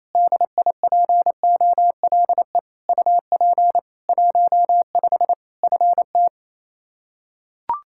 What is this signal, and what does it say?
Morse code, heard as a keyed tone, about 28 wpm, 700 Hz. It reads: DIPOLE UP 15 FT
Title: K1USN SST QSO Practice 28wpm